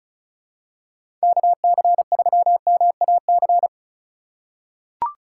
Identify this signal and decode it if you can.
Morse code, heard as a keyed tone, about 36 wpm, 700 Hz. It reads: KC3MAC
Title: USA Call Signs - 6 Characters - 3x Character Spacing - 35wpm